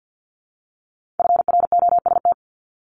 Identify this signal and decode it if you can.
Morse code, heard as a keyed tone, about 50 wpm, 700 Hz. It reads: FROST
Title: Multi-letter Phonemes Repeated 3x 50wpm